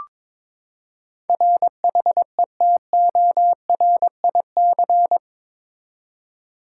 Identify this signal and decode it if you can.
Morse code, heard as a keyed tone, about 22 wpm, 700 Hz. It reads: RHETORIC
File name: practice / 22wpm / Eight Letter Words - 22wpm